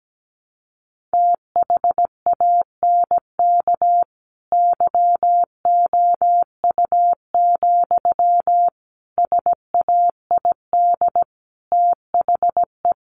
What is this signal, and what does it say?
Morse code, heard as a keyed tone, about 17 wpm, 700 Hz. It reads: THANK YOU, SAID THE
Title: Sentences from Top 1100 Words 17wpm